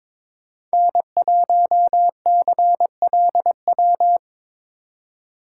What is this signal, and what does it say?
Morse code, heard as a keyed tone, about 24 wpm, 700 Hz. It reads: N1CLW
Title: Speed Racing - USA Call Signs 22wpm